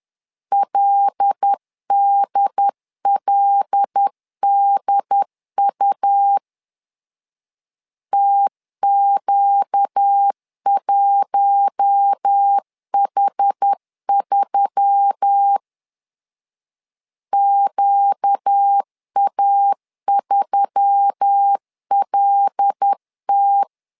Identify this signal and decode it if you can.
Morse code, heard as a keyed tone, about 10 wpm, 800 Hz. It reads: LDLDU TQ1H3 QA3LT
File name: LEZIONI / 5 LBVC / 5 L 10wpm 2259